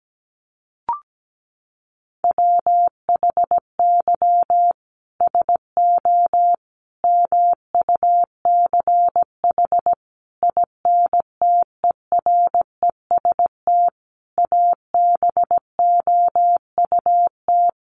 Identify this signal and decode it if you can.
Morse code, heard as a keyed tone, about 17 wpm, 700 Hz. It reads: WHY SO MUCH INTEREST ABOUT
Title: Sentences from Top 1200 Words 17wpm